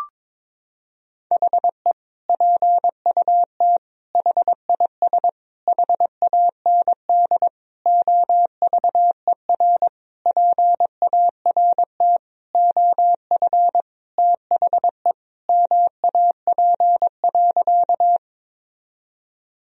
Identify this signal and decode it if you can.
Morse code, heard as a keyed tone, about 22 wpm, 700 Hz. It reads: HE PUT HIS HAND OVER PART OF THE MAP.
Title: Sentences from Top 1000 Words - Review Part 2 - 22wpm